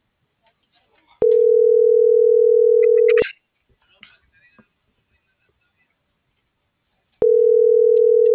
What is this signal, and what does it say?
Ringback tone (North America), 440 + 480 Hz.